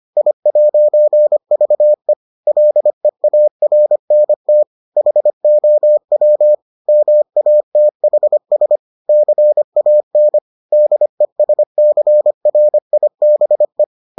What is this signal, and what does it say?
Morse code, heard as a keyed tone, about 25 wpm, 600 Hz. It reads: I'VE LEARNT HOW MATHS CAN DESCRIBE